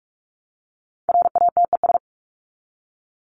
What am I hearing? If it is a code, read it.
Morse code, heard as a keyed tone, about 44 wpm, 700 Hz. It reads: RATES
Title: ES:S Suffix - Top 100 Words with Base Words 45wpm